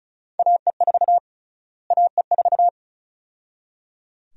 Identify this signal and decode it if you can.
Morse code, heard as a keyed tone, about 36 wpm, 700 Hz. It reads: AE4 AE4
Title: USA Call Sign Prefixes - Repeated 2x - 35wpm